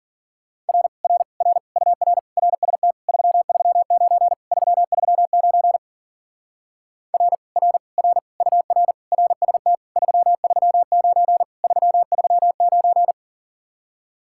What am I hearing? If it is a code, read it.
Morse code, heard as a keyed tone, about 50 wpm, 700 Hz. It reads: R R R UR RST 339 339 R R R UR RST 339 339
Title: Speed-Racing Common QSO Phrases Encore - 40wpm